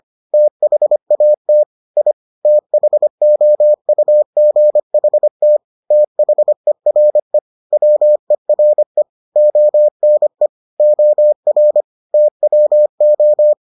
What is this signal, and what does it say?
Morse code, heard as a keyed tone, about 25 wpm, 600 Hz. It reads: THAT I THOUGHT THERE WERE ONE OR TWO